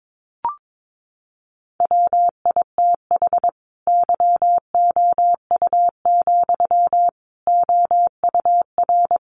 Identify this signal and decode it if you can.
Morse code, heard as a keyed tone, about 22 wpm, 700 Hz. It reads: WITH YOU, OUR
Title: Sentences from Top 1400 Words 22wpm